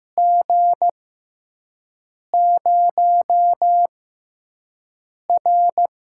Morse code, heard as a keyed tone, about 15 wpm, spaced as for 5 wpm, 700 Hz. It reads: G0R
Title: USA Call Signs - 4 Characters - 6x Character Spacing - 15wpm